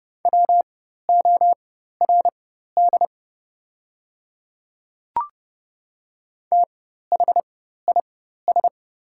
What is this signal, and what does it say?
Morse code, heard as a keyed tone, about 30 wpm, spaced as for 14 wpm, 700 Hz. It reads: WORD THIS
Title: 4x Character Spacing - Top 100 Words - 30wpm